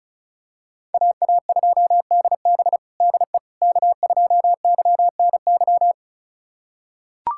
Morse code, heard as a keyed tone, about 34 wpm, 700 Hz. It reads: AA2DB DE K2YNY
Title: USA Callsign DE Callsign - 5 Characters - 35wpm